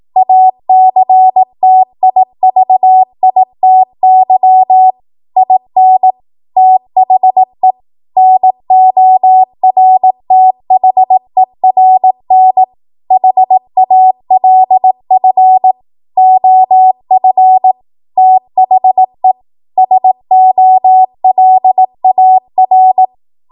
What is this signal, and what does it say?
Morse code, heard as a keyed tone, about 18 wpm, 750 Hz. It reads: ACTIVITY IN THE NORTHERN HALF OF THE SOLAR
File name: Morse / 18WPM